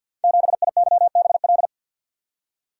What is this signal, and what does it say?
Morse code, heard as a keyed tone, about 51 wpm, 700 Hz. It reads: 7IYBL
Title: Code Groups - 5 Letter:Numbers - 50wpm